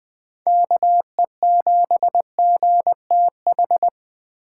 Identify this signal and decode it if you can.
Morse code, heard as a keyed tone, about 20 wpm, 700 Hz. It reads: KE7GTH
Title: USA Call Signs - 8x Character Spacing - 20wpm